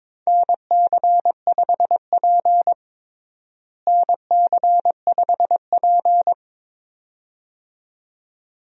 Morse code, heard as a keyed tone, about 22 wpm, 700 Hz. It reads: NC5P NC5P
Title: USA Call Signs - 4 Characters - Repeated 2x - 22wpm